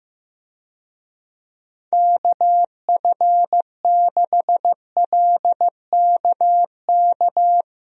Morse code, heard as a keyed tone, about 15 wpm, 700 Hz. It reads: KF6LKK
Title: USA Call Signs - 6 Characters - Repeated 2x - 15wpm